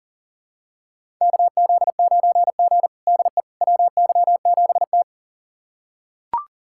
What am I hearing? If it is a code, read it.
Morse code, heard as a keyed tone, about 37 wpm, 700 Hz. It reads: KZ9G DE WY7T
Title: USA Callsign DE Callsign - 4 Characters - 40wpm